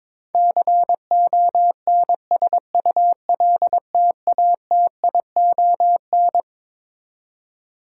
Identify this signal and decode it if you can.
Morse code, heard as a keyed tone, about 22 wpm, 700 Hz. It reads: CONSULTATION
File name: Twelve Letter Words - 22wpm